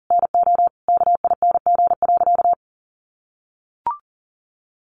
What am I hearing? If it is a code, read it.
Morse code, heard as a keyed tone, about 40 wpm, 700 Hz. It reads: NO KING.
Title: Sentences from Top 600 Words 40wpm